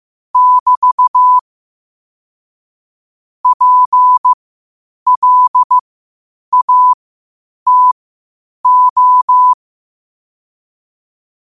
Morse code, heard as a keyed tone, about 15 wpm, spaced as for 8 wpm, 1000 Hz. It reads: <BT> PLATO
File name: qotdfiles / 8wpm